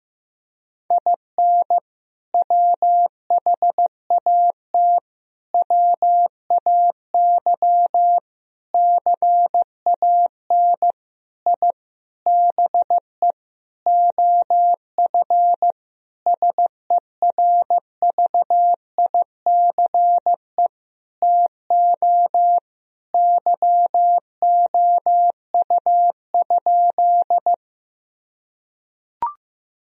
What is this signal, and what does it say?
Morse code, heard as a keyed tone, about 15 wpm, 700 Hz. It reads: IN WHAT WAY CAN I BE OF SERVICE TO YOU?
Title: Sentences from Top 1000 Words - Review Part 4 - 15wpm